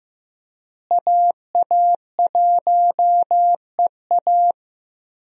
Morse code, heard as a keyed tone, about 15 wpm, 700 Hz. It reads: AA1EA
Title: USA Call Signs - 5 Characters - Repeated 3x - 15wpm